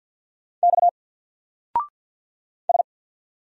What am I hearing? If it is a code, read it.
Morse code, heard as a keyed tone, about 48 wpm, 700 Hz. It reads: X S
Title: Single Character (Letter:Number:Punctuation:Prosign) 50wpm